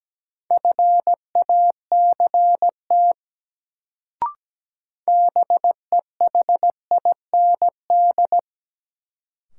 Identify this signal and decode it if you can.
Morse code, heard as a keyed tone, about 17 wpm, 700 Hz. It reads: FACT BEHIND
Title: Sentences from Top 500 Words - New Word Drill - 17wpm